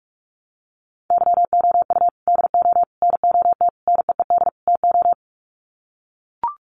Morse code, heard as a keyed tone, about 44 wpm, 700 Hz. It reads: YOU DO NOT NEED TO
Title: Sets of 5 Words 45wpm